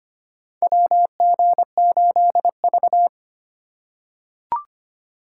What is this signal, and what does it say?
Morse code, heard as a keyed tone, about 25 wpm, 700 Hz. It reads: WG8V
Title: USA Call Signs - 4 Characters - 4x Character Spacing - 25wpm